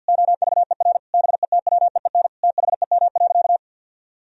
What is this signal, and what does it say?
Morse code, heard as a keyed tone, about 47 wpm, 700 Hz. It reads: OVER BETWEEN THEM.
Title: Sentences from Top 1000 Words - Review Part 1 - 50wpm